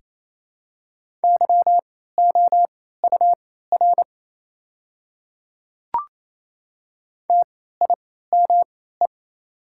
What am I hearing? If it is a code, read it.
Morse code, heard as a keyed tone, about 27 wpm, spaced as for 16 wpm, 700 Hz. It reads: YOUR TIME
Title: 3x Character Spacing - Top 100 Words - 28wpm